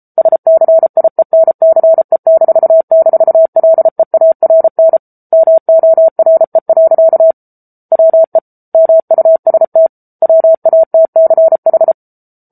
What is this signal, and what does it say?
Morse code, heard as a keyed tone, about 34 wpm, 650 Hz. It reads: SCIENCE--LEARN MORE. WE MUST WATCH
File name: Book Two / 07-Chapter 7 - 33 WPM